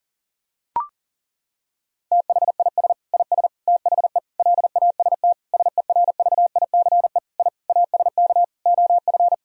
Morse code, heard as a keyed tone, about 40 wpm, 700 Hz. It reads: THIS IS THE LAST SERVICE I ASK OF
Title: Sentences from Top 1000 Words - Review Part 4 - 40wpm